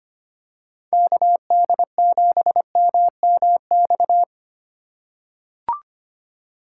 Morse code, heard as a keyed tone, about 25 wpm, 700 Hz. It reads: KD7MMX
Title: USA Call Signs - 6 Characters - 2x Character Spacing - 25wpm